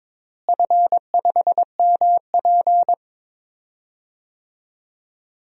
Morse code, heard as a keyed tone, about 22 wpm, 700 Hz. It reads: F5MP
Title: DXCC-100 Call Signs - 4 Characters - Repeated 2x - 22wpm